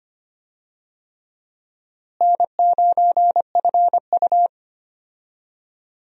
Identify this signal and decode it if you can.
Morse code, heard as a keyed tone, about 25 wpm, 700 Hz. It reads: N9FU
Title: USA Call Signs - 4 Characters - Repeated 3x - 25wpm